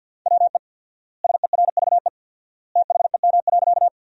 Morse code, heard as a keyed tone, about 51 wpm, 700 Hz. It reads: WE SERVE THEM.
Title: 4x Word-Spacing - Sentences from Top 400 Words - 50wpm